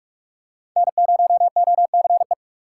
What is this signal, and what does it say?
Morse code, heard as a keyed tone, about 46 wpm, 700 Hz. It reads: N0OCE
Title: USA Call Signs - 5 Characters - 3x Character Spacing - 45wpm